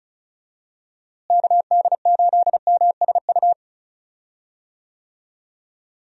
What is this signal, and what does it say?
Morse code, heard as a keyed tone, about 36 wpm, 700 Hz. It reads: KD8MSU
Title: USA Call Signs - 6 Characters - Repeated 3x - 35wpm